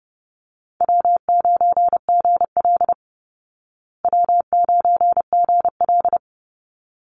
Morse code, heard as a keyed tone, about 30 wpm, 700 Hz. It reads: W9GL W9GL
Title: Easy Call Signs Repeated 2x - 30wpm